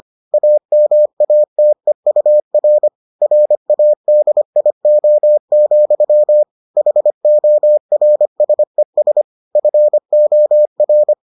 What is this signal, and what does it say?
Morse code, heard as a keyed tone, about 25 wpm, 600 Hz. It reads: AMATEUR RADIO, HORSES FOR